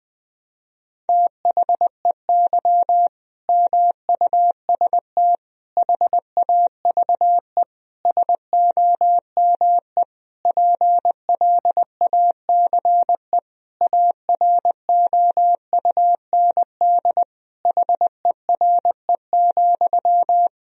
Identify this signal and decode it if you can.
Morse code, heard as a keyed tone, about 20 wpm, 700 Hz. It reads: THEY MUST HAVE SOME PLACE AROUND HERE,
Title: Sentences from Top 1000 Words - Review Part 3 - 20wpm